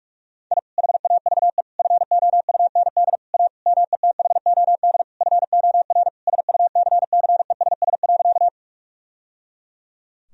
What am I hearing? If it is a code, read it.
Morse code, heard as a keyed tone, about 41 wpm, 700 Hz. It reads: I HAVE FOUND A METHOD FOR SUCCESS.
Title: Sentences from Top 1000 Words - Review Part 3 - 45wpm